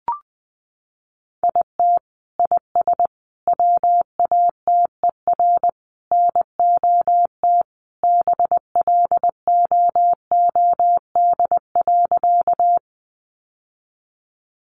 Morse code, heard as a keyed tone, about 20 wpm, 700 Hz. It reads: IT IS WATER NOT BLOOD.
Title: Sentences from Top 1100 Words 20wpm